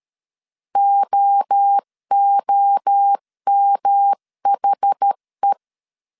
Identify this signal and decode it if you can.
Morse code, heard as a keyed tone, about 13 wpm, 800 Hz. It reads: OOMHE
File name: CW / DET / 1 3 1500C S A 12wpm 2905